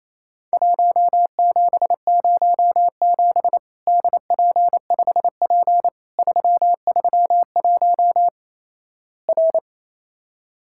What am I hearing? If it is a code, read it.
Morse code, heard as a keyed tone, about 28 wpm, 700 Hz. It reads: 1707 DP5P 331 R
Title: Worked All Europe DX Contest - QTC Exchange - 28wpm